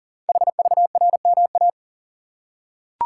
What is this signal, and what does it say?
Morse code, heard as a keyed tone, about 39 wpm, 700 Hz. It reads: HVRMA
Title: Code Groups - 5 Letters - 40wpm